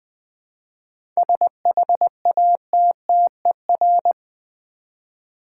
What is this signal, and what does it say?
Morse code, heard as a keyed tone, about 20 wpm, 700 Hz. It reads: SHATTER
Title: 7 Letter Words - First 3 Letters Given - 20wpm